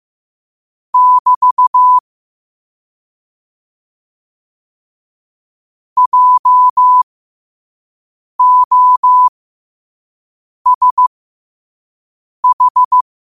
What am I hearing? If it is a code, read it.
Morse code, heard as a keyed tone, about 15 wpm, spaced as for 5 wpm, 1000 Hz. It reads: <BT> JOSH
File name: qotdfiles / 5wpm